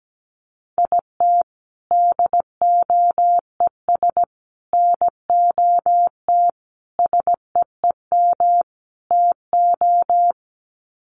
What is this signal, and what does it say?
Morse code, heard as a keyed tone, about 17 wpm, 700 Hz. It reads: IT DOES NOT SEEM TO
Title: Sets of 5 Words Encore2 17wpm